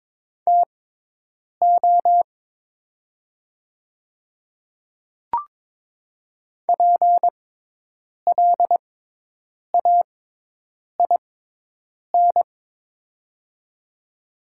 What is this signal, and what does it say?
Morse code, heard as a keyed tone, about 22 wpm, spaced as for 8 wpm, 700 Hz. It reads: TO PLAIN
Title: 6x Character Spacing - Top 500 Words - 22wpm